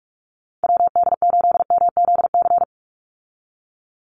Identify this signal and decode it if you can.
Morse code, heard as a keyed tone, about 44 wpm, 700 Hz. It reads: WD8MZC
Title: USA Call Signs 45wpm